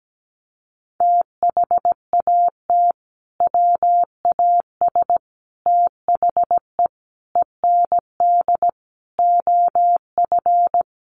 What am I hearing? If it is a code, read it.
Morse code, heard as a keyed tone, about 17 wpm, 700 Hz. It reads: THAT WAS THE END OF
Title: Sentences from Top 1400 Words 17wpm